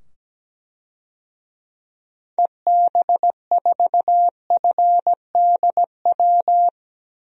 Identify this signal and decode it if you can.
Morse code, heard as a keyed tone, about 17 wpm, 700 Hz. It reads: EB4FDW
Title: DXCC-100 Call Signs - 6 Characters - 17wpm